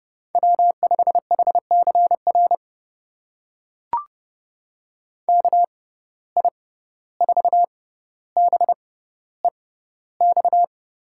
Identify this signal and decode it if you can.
Morse code, heard as a keyed tone, about 30 wpm, spaced as for 15 wpm, 700 Hz. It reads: W5HCR KI4BEX
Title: USA Call Signs - 6x Character Spacing - 30wpm